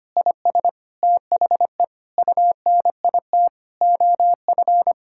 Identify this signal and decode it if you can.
Morse code, heard as a keyed tone, about 25 wpm, 700 Hz. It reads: IS THE UNIT OF
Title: Sentences from Top 1000 Words - Review Part 6 - 25wpm